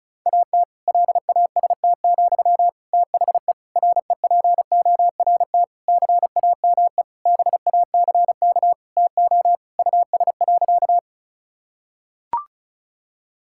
Morse code, heard as a keyed tone, about 36 wpm, 700 Hz. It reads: AT LAST, THE REPORT CAME BACK TO US.
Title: Sentences from Top 1000 Words - Review Part 6 - 35wpm